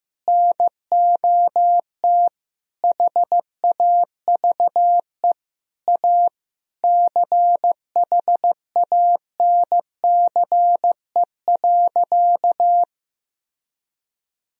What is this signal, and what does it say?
Morse code, heard as a keyed tone, about 15 wpm, 700 Hz. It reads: NOT HAVE A CHANCE.
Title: Sentences from Top 1000 Words - Review Part 2 - 15wpm